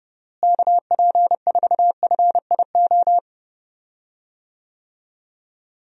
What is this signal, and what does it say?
Morse code, heard as a keyed tone, about 30 wpm, 700 Hz. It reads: KP4FIO
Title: USA Call Signs - 6 Characters - Repeated 3x - 30wpm